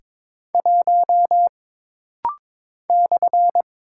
Morse code, heard as a keyed tone, about 22 wpm, 700 Hz. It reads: 1 /
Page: This rapid-fire timing is geared to push your ICR (Instant Character Recognition) skills.